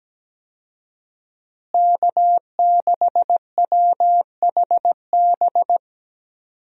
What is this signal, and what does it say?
Morse code, heard as a keyed tone, about 17 wpm, 700 Hz. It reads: K6WHB
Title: USA Call Signs - 5 Characters - Repeated 2x - 17wpm